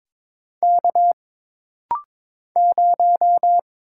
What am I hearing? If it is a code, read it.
Morse code, heard as a keyed tone, about 22 wpm, 700 Hz. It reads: K 0
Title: Single Letter-Number - Rapid-Fire 22wpm